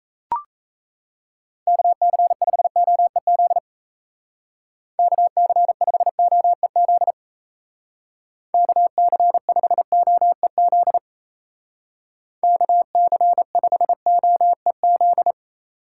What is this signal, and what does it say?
Morse code, heard as a keyed tone, about 34 wpm, 700 Hz. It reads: KC5OEZ KC5OEZ KC5OEZ KC5OEZ
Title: Speed Racing - USA 2x3 Call Signs 28wpm